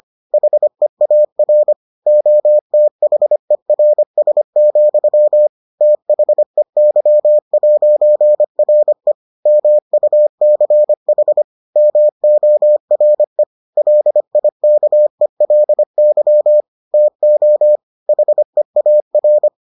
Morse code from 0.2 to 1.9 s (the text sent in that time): HEAR